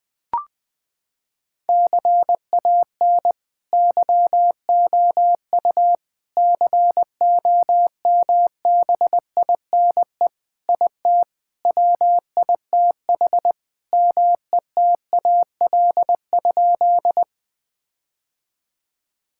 Morse code, heard as a keyed tone, about 20 wpm, 700 Hz. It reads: CAN YOU COMBINE IT WITH METAL?
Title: Sentences from Top 1600 Words 20wpm